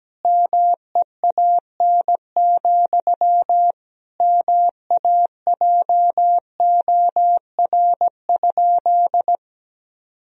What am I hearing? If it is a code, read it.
Morse code, heard as a keyed tone, about 17 wpm, 700 Hz. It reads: MEAN, MAJOR?
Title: Sentences from Top 1000 Words - Review Part 4 - 17wpm